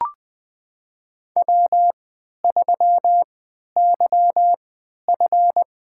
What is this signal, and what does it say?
Morse code, heard as a keyed tone, about 21 wpm, spaced as for 12 wpm, 700 Hz. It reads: W3YF
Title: USA Call Signs - 4 Characters - 3x Character Spacing - 20wpm